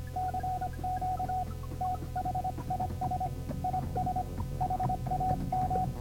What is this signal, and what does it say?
Morse code, heard as a keyed tone, about 25 wpm, 740 Hz, with noise. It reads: CQ THIS IS HAM